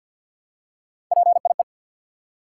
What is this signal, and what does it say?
Morse code, heard as a keyed tone, about 47 wpm, 700 Hz. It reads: PIE
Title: Three Letter Words - 50wpm